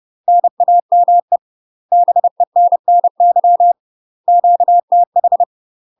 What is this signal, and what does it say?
Morse code, heard as a keyed tone, about 30 wpm, 700 Hz. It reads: NAME BENNY QTH
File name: UK MASTER QSOs TRAINING / 30WPM / UK MASTER QSOs 0043